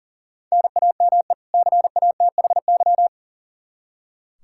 Morse code, heard as a keyed tone, about 40 wpm, 700 Hz. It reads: NAME CATHY
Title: Practice QSO - Exchange 1 - Beginning - 40wpm